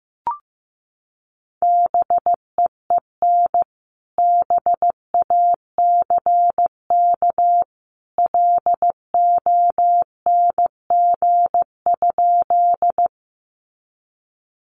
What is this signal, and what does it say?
Morse code, heard as a keyed tone, about 15 wpm, 700 Hz. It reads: BEEN BACK LONG?
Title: Sentences from Top 200 Words 15wpm